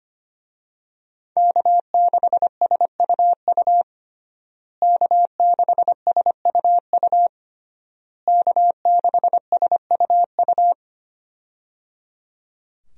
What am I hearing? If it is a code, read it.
Morse code, heard as a keyed tone, about 25 wpm, 700 Hz. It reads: K6SUU K6SUU K6SUU
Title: USA Call Signs - 5 Characters - Repeated 3x - 25wpm